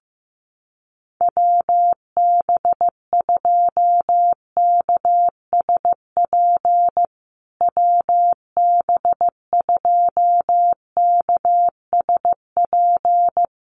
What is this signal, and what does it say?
Morse code, heard as a keyed tone, about 15 wpm, 700 Hz. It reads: WB2KSP WB2KSP
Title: USA Call Signs Repeated 3x 15wpm